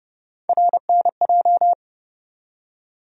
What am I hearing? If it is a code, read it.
Morse code, heard as a keyed tone, about 30 wpm, 700 Hz. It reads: RNJ
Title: USA Call Sign Postfixes - 3 Letters 30wpm